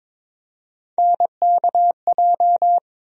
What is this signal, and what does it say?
Morse code, heard as a keyed tone, about 22 wpm, 700 Hz. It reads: NKJ
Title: USA Call Sign Postfixes - 3 Letters 22wpm